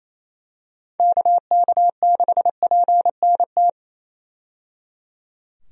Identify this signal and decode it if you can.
Morse code, heard as a keyed tone, about 28 wpm, 700 Hz. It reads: KK6PNT
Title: USA Call Signs - 6 Characters - 28wpm